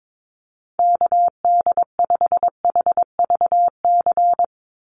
Morse code, heard as a keyed tone, about 22 wpm, 700 Hz. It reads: KD5HVC
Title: USA Call Signs 22wpm